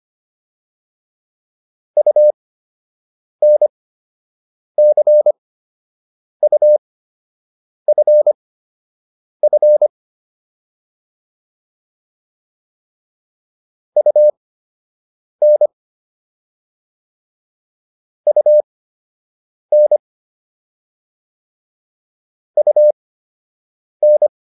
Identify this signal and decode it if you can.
Morse code, heard as a keyed tone, about 25 wpm, spaced as for 7 wpm, 600 Hz. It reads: UNCUFF UN UN UN